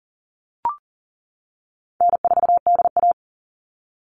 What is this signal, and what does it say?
Morse code, heard as a keyed tone, about 40 wpm, 700 Hz. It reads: N4DA
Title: NA SOTA Chaser Callsigns 40wpm